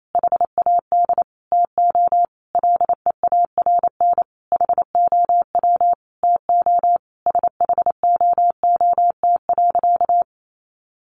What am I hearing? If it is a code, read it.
Morse code, heard as a keyed tone, about 28 wpm, 700 Hz. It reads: HAD TO LEARN HOW TO SHOOT.
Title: Sentences from Top 1200 Words 28wpm